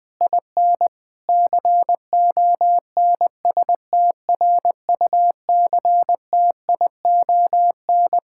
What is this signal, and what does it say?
Morse code, heard as a keyed tone, about 20 wpm, 700 Hz. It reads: IN CONSTRUCTION
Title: Sentences from Top 1600 Words 20wpm